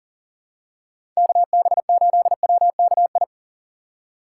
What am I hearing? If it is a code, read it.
Morse code, heard as a keyed tone, about 37 wpm, 700 Hz. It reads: KB8WKI
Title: USA Call Signs - 6 Characters - 4x Character Spacing - 40wpm